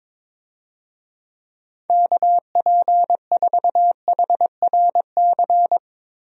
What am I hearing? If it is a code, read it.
Morse code, heard as a keyed tone, about 22 wpm, 700 Hz. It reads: KP4HRC
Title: USA Call Signs - 6 Characters - Repeated 2x - 22wpm